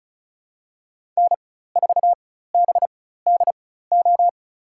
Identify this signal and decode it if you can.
Morse code, heard as a keyed tone, about 34 wpm, spaced as for 16 wpm, 700 Hz. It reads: N4BDO
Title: USA Call Signs - 5 Characters - 4x Character Spacing - 35wpm